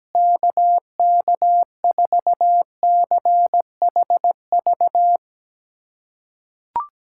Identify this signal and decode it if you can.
Morse code, heard as a keyed tone, about 17 wpm, 700 Hz. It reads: KK4CHV
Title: USA Callsign DE Callsign 17wpm